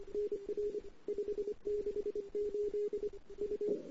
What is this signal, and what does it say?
Morse code, heard as a keyed tone, about 26 wpm, 410 Hz. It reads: NR 568 U
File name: CW / 2008-ss-cw-k5tr-op-n5rz -08:00